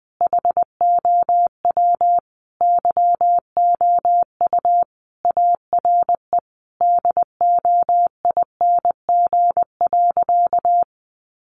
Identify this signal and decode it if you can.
Morse code, heard as a keyed tone, about 20 wpm, 700 Hz. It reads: HOW YOU ARE DOING.
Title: Sentences from Top 1400 Words 20wpm